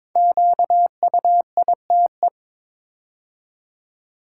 Morse code, heard as a keyed tone, about 22 wpm, 700 Hz. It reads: QUITE
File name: 5 Letter Words - First 3 Letters Given - 22wpm